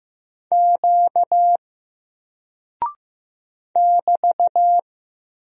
Morse code, heard as a keyed tone, about 15 wpm, 700 Hz. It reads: Q <BT>
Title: Single Character (Letter:Number:Punctuation:Prosign) 15wpm